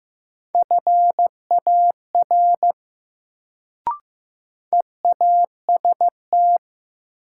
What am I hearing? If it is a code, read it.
Morse code, heard as a keyed tone, about 15 wpm, 700 Hz. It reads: FAR EAST
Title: Sets of 3 Words - New Word Drill - 15wpm